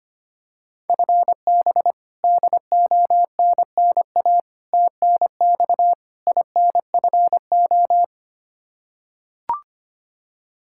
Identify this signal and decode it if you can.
Morse code, heard as a keyed tone, about 25 wpm, 700 Hz. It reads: FB DONNA TNX INFO
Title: Practice QSO - Exchange 3 - More Info - 25wpm